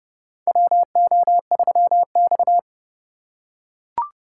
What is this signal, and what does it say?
Morse code, heard as a keyed tone, about 30 wpm, 700 Hz. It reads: WO3X
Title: USA Call Signs - 4 Characters - 4x Character Spacing - 30wpm